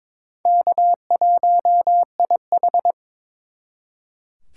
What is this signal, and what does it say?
Morse code, heard as a keyed tone, about 26 wpm, 700 Hz. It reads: K1IH
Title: Speed Racing - USA 1x2 Call Signs 22wpm